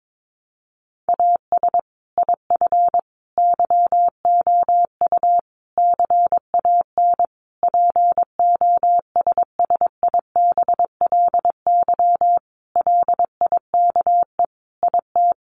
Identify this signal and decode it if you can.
Morse code, heard as a keyed tone, about 22 wpm, 700 Hz. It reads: AS IF YOU CAN POSSIBLY LIKE IT
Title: Sentences from Top 1300 Words 22wpm